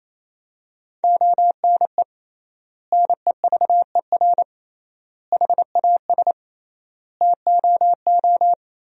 Morse code, heard as a keyed tone, about 28 wpm, 700 Hz. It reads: ONE NEVER HAS TOO
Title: 3x Word-Spacing - Sentences from Top 300 Words - 28wpm